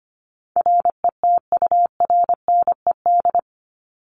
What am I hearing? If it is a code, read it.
Morse code, heard as a keyed tone, about 25 wpm, 700 Hz. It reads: RETURNED
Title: ED Suffix - Top 100 Words with Base Words 25wpm